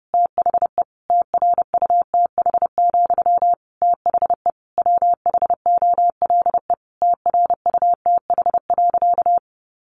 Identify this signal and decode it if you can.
Morse code, heard as a keyed tone, about 30 wpm, 700 Hz. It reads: THE TRUTH, THE WHOLE TRUTH.
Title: Sentences from Top 800 Words 30wpm